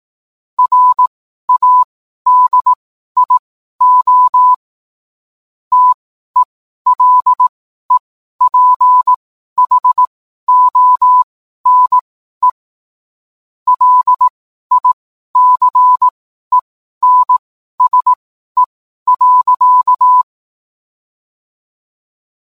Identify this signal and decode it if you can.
Morse code, heard as a keyed tone, about 18 wpm, spaced as for 12 wpm, 1000 Hz. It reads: RADIO TELEPHONE LICENSE.